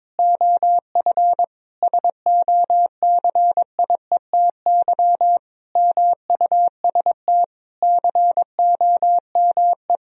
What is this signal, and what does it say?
Morse code, heard as a keyed tone, about 22 wpm, 700 Hz. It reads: OF SOCIETY MUST COME
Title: Sentences from Top 1000 Words - Review Part 4 - 22wpm